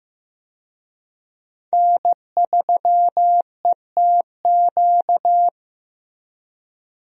Morse code, heard as a keyed tone, about 15 wpm, 700 Hz. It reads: N3ETQ
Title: USA Call Signs - 5 Characters - Repeated 2x - 15wpm